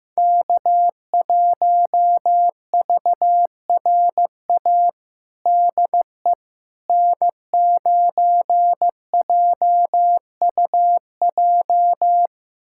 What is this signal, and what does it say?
Morse code, heard as a keyed tone, about 15 wpm, 700 Hz. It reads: K1VRA DE N9JUJ
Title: USA Callsign DE Callsign - 5 Characters - 15wpm